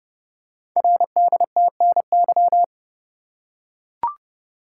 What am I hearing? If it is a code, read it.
Morse code, heard as a keyed tone, about 29 wpm, 700 Hz. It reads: RDTNY
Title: Code Groups - 5 Letters - 30wpm